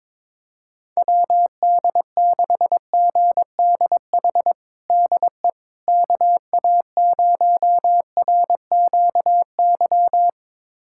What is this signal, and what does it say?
Morse code, heard as a keyed tone, about 22 wpm, 700 Hz. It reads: WD6GDH DE KA0RQY
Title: USA Callsign DE Callsign - 6 Characters - 22wpm